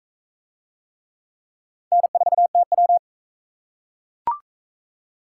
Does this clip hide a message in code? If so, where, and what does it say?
Yes: Morse code from 1.9 to 3.0 s, heard as a keyed tone, about 44 wpm, 700 Hz. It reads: N4TW